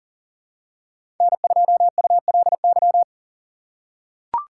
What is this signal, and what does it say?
Morse code, heard as a keyed tone, about 40 wpm, 700 Hz. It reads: N2ULY
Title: USA Call Signs - 5 Characters - 4x Character Spacing - 40wpm